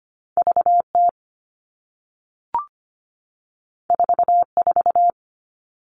Morse code, heard as a keyed tone, about 25 wpm, 700 Hz. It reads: VT 44
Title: All Cut Numbers - 2-Digit 25wpm